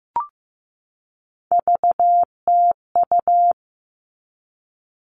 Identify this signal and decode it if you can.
Morse code, heard as a keyed tone, about 15 wpm, 700 Hz. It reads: VTU
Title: All Cut Numbers - 3-Digit 15wpm